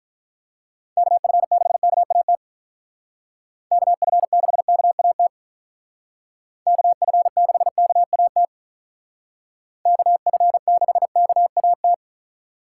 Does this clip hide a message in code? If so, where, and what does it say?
Yes: Morse code from 1.0 to 11.9 s, heard as a keyed tone, about 45 wpm, 700 Hz. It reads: KF6KAT KF6KAT KF6KAT KF6KAT